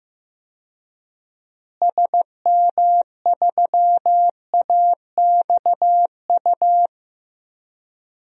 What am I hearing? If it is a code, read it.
Morse code, heard as a keyed tone, about 15 wpm, 700 Hz. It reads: SM3AXU
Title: DXCC-100 Call Signs - 6 Characters - Repeated 3x - 15wpm